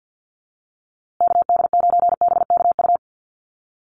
Morse code, heard as a keyed tone, about 50 wpm, 700 Hz. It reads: KD9BKU
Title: USA Call Signs Repeated 2x 50wpm